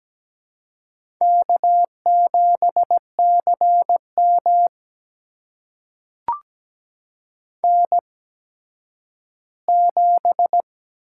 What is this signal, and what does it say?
Morse code, heard as a keyed tone, about 17 wpm, spaced as for 10 wpm, 700 Hz. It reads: K7CM N7
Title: USA Call Signs - 4 Characters - 8x Character Spacing - 17wpm